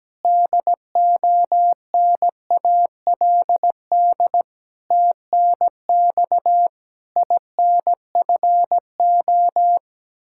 Morse code, heard as a keyed tone, about 17 wpm, 700 Hz. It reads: DONALD TNX INFO
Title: Practice QSO - Exchange 3 - More Info - 17wpm